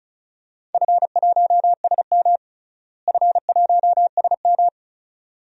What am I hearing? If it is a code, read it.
Morse code, heard as a keyed tone, about 36 wpm, 700 Hz. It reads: F1SM F1SM
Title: DXCC-100 Call Signs - 4 Characters - Repeated 2x - 35wpm